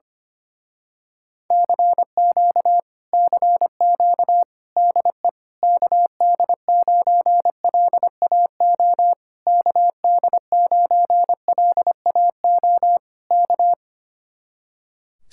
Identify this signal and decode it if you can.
Morse code, heard as a keyed tone, about 24 wpm, 700 Hz. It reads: CQ CQ DE KD9LAO KD9LAO K
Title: Speed-Racing Common QSO Phrases - 25wpm